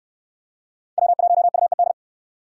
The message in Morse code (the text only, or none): W1UED